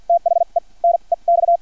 Morse code, heard as a keyed tone, about 50 wpm, spaced as for 39 wpm, 680 Hz. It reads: THE NEX
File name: archived / CRSnet